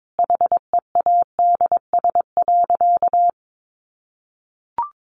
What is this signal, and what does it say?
Morse code, heard as a keyed tone, about 22 wpm, 700 Hz. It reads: HEADS.
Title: A Sound Of Thunder 22wpm